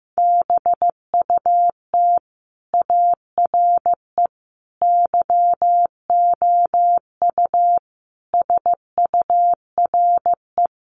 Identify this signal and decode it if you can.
Morse code, heard as a keyed tone, about 15 wpm, 700 Hz. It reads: BUT ARE YOU SURE
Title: Sentences from Top 300 Words 15wpm